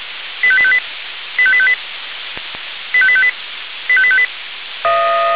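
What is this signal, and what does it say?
Five-tone selective call (ZVEI), heard as quick runs of five tones, tones of about 70 ms. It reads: 85858 85858 85858 85858
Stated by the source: ZVEI five-tone selective call received on an SDR.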